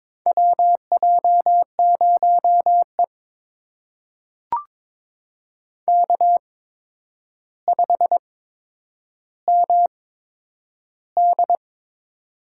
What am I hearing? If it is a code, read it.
Morse code, heard as a keyed tone, about 22 wpm, spaced as for 9 wpm, 700 Hz. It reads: WJ0E K5MD
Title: USA Call Signs - 4 Characters - 8x Character Spacing - 22wpm